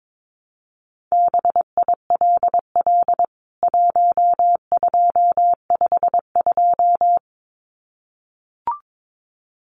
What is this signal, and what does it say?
Morse code, heard as a keyed tone, about 22 wpm, 700 Hz. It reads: BILL 1252
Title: CWT Exchange 22wpm